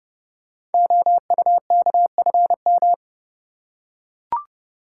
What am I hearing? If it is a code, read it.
Morse code, heard as a keyed tone, about 30 wpm, 700 Hz. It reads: OUKFM
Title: Code Groups - 5 Letters - 30wpm